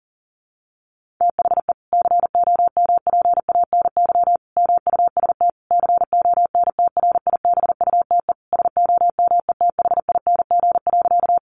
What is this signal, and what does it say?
Morse code, heard as a keyed tone, about 40 wpm, 700 Hz. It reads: THE COMPANY MUST CONTRIBUTE SOMETHING.
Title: Sentences from Top 1400 Words 40wpm